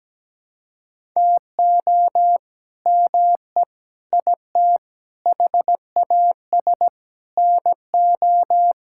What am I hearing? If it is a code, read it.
Morse code, heard as a keyed tone, about 17 wpm, 700 Hz. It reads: TO ME IT HAS NO
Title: Sentences from Top 1000 Words - Review Part 6 - 17wpm